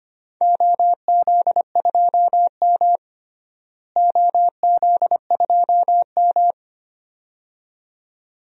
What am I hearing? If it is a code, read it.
Morse code, heard as a keyed tone, about 25 wpm, 700 Hz. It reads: OZ2M OZ2M
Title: DXCC-100 Call Signs - 4 Characters - Repeated 2x - 25wpm